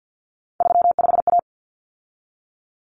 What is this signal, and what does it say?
Morse code, heard as a keyed tone, about 50 wpm, 700 Hz. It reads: 35A
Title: Common Cut Numbers - 3-Digit 50wpm